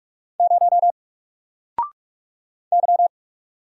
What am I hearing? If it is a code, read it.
Morse code, heard as a keyed tone, about 46 wpm, 700 Hz. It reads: 0 Y
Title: Single Letter-Number 45wpm